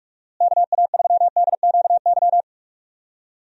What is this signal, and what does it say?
Morse code, heard as a keyed tone, about 42 wpm, 700 Hz. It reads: KA3DQY
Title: USA Call Signs - 6 Characters - Repeated 3x - 45wpm